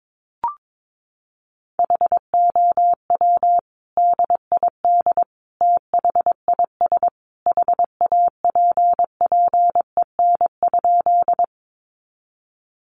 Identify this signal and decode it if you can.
Morse code, heard as a keyed tone, about 22 wpm, 700 Hz. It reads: HOW DID THIS HAPPEN?
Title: Sentences from Top 400 Words 22wpm